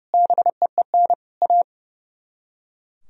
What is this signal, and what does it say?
Morse code, heard as a keyed tone, about 37 wpm, 700 Hz. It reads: BEEN A
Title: Speed-Racing - Sets of 4 Words 30wpm